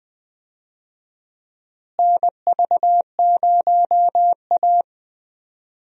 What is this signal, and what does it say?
Morse code, heard as a keyed tone, about 21 wpm, 700 Hz. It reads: NV0A
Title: USA Call Signs - 4 Characters - Repeated 2x - 20wpm